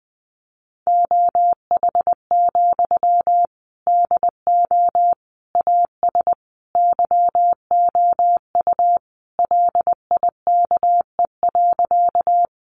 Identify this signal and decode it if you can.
Morse code, heard as a keyed tone, about 20 wpm, 700 Hz. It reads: OH, DO AS YOU LIKE.
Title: Sentences from Top 500 Words 20wpm